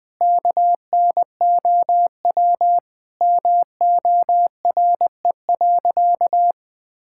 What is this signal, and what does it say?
Morse code, heard as a keyed tone, about 20 wpm, 700 Hz. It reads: KNOW MORE.
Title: Sentences with 6-Word Phrases - 20wpm